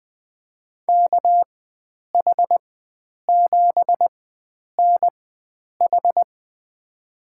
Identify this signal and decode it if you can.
Morse code, heard as a keyed tone, about 20 wpm, spaced as for 9 wpm, 700 Hz. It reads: KH7NH
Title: USA Call Signs - 5 Characters - 4x Character Spacing - 20wpm